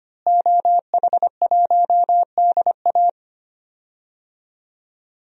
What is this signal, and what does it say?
Morse code, heard as a keyed tone, about 25 wpm, 700 Hz. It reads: OH1DA
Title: DXCC-100 Call Signs - 5 Characters - Repeated 2x - 25wpm